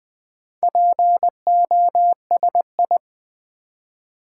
Morse code, heard as a keyed tone, about 20 wpm, 700 Hz. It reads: POSI
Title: Build A Word - 2 Letters At A Time - 8-Letter Words - 20wpm